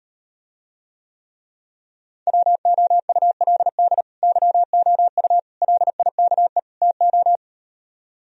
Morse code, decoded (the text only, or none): WOULD YOU LIKE TO